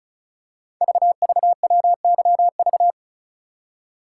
Morse code, heard as a keyed tone, about 36 wpm, 690 Hz. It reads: VVWYV
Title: Code Groups - 5 Letters - 35wpm